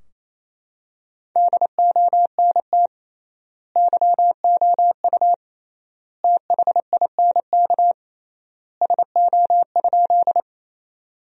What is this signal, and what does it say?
Morse code, heard as a keyed tone, about 27 wpm, 700 Hz. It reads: DONT YOU THINK SO?
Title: 3x Word-Spacing - Sentences from Top 300 Words - 28wpm